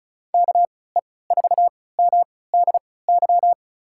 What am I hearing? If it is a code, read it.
Morse code, heard as a keyed tone, about 36 wpm, spaced as for 20 wpm, 700 Hz. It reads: KE4MDY
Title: USA Call Signs - 6 Characters - 3x Character Spacing - 35wpm